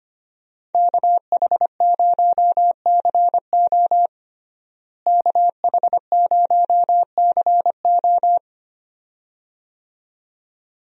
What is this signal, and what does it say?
Morse code, heard as a keyed tone, about 25 wpm, 700 Hz. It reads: KH0CO KH0CO
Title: USA Call Signs - 5 Characters - Repeated 2x - 25wpm